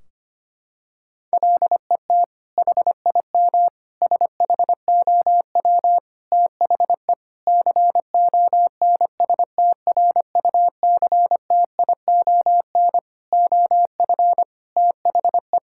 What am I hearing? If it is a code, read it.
Morse code, heard as a keyed tone, about 25 wpm, 700 Hz. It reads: LET HIM SHOW THE CONSTRUCTION OF THE
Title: Sentences from Top 1600 Words 25wpm